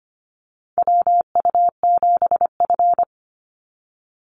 Morse code, heard as a keyed tone, about 25 wpm, 700 Hz. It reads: WU7F
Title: Call Signs - 25wpm